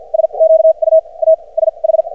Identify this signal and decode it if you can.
Morse code, heard as a keyed tone, about 53 wpm, spaced as for 33 wpm, 590 Hz. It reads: SQUASH
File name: archived / CRSnet